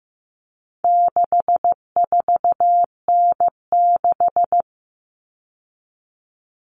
Morse code, heard as a keyed tone, about 15 wpm, 700 Hz. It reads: 64N6
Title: All Cut Numbers - 4-Digit 15wpm